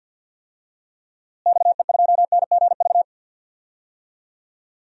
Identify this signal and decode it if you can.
Morse code, heard as a keyed tone, about 51 wpm, 700 Hz. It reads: XE2NGV
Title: DXCC-100 Call Signs - 6 Characters - Repeated 3x - 50wpm